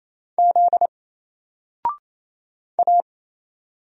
Morse code, heard as a keyed tone, about 27 wpm, 700 Hz. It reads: Z A
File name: Single-Letter-Number-28wpm